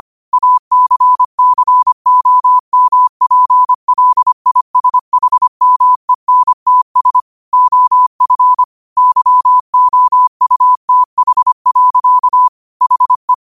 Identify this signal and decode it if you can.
Morse code, heard as a keyed tone, about 25 wpm, 1000 Hz. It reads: ACCOMPLISHMENTS OF YOUTH. HE
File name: qotdfiles / 25wpm